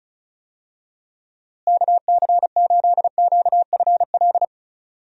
Morse code, heard as a keyed tone, about 36 wpm, 700 Hz. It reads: KC8QFL
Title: USA Call Signs - 6 Characters - Repeated 3x - 35wpm